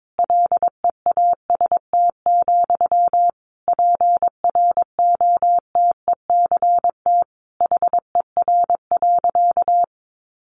Morse code, heard as a keyed tone, about 22 wpm, 700 Hz. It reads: LEAST, PROTECT HER.
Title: Sentences from Top 900 Words 22wpm